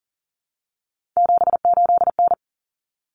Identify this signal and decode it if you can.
Morse code, heard as a keyed tone, about 40 wpm, 700 Hz. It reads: 78N
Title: Common Cut Numbers - 3-Digit 40wpm